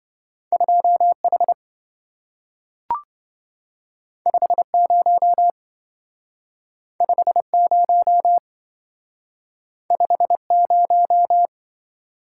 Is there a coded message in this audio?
Yes: Morse code from 0.5 to 11.5 s, heard as a keyed tone, about 27 wpm, 700 Hz. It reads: 2H 50 50 50